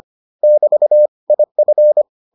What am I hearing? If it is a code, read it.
Morse code, heard as a keyed tone, about 25 wpm, 600 Hz. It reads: <BT> IF